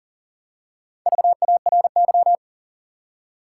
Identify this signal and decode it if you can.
Morse code, heard as a keyed tone, about 40 wpm, 700 Hz. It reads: VARY
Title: Four Letter Words - 40wpm